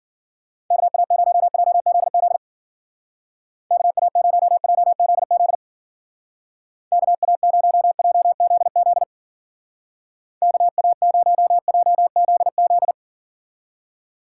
Morse code, heard as a keyed tone, about 50 wpm, 700 Hz. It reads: KA0JZZ KA0JZZ KA0JZZ KA0JZZ